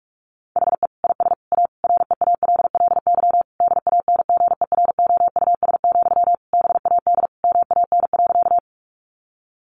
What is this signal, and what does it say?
Morse code, heard as a keyed tone, about 44 wpm, 700 Hz. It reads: HE IS A REALLY DANGEROUS, BAD MAN.
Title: Sentences from Top 1200 Words 45wpm